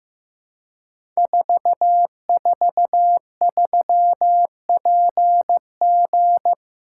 Morse code, heard as a keyed tone, about 15 wpm, 700 Hz. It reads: 443PG
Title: Code Groups - 5 Letter:Numbers - 15wpm